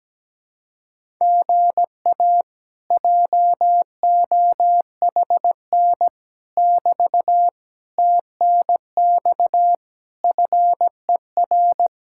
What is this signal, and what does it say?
Morse code, heard as a keyed tone, about 17 wpm, 700 Hz. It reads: GA JOHN <BT> TNX FER
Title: Speed-Racing Common QSO Phrases Encore - 17wpm